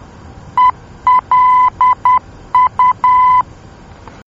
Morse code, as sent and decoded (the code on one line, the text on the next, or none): . .-.. ..-
ELU